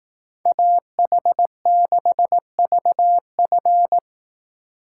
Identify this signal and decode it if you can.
Morse code, heard as a keyed tone, about 18 wpm, 700 Hz. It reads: AH6VF
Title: Speed Racing - USA Call Signs 15wpm